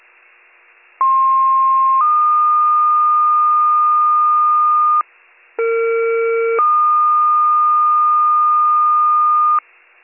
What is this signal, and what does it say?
Two-tone page, heard as a one-second tone, then a three-second tone, about 1060 then 1220 Hz, about 460 then 1220 Hz. It is heard from 1.0 to 9.6 s.